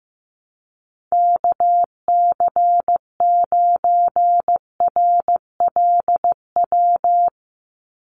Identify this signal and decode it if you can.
Morse code, heard as a keyed tone, about 14 wpm, 700 Hz. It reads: KC9RLW
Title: USA Call Signs 15wpm